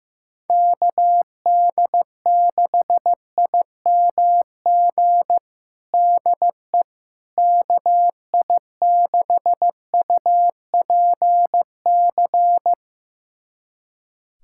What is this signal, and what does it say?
Morse code, heard as a keyed tone, about 15 wpm, 700 Hz. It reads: KD6IMG DE KI6UPC
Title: USA Callsign DE Callsign 15wpm